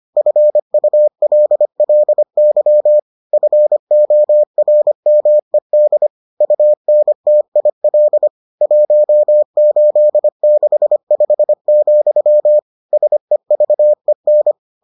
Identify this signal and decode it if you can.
Morse code, heard as a keyed tone, about 25 wpm, 600 Hz. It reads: FULLY FORMED UNTIL 1865, SEVEN